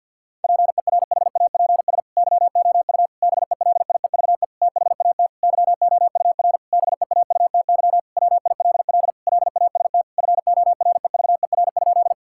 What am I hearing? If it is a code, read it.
Morse code, heard as a keyed tone, about 50 wpm, 700 Hz. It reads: PERHAPS YOU BELIEVE THAT YOUR BEAUTY WILL LAST FOREVER?
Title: Sentences from Top 1600 Words 50wpm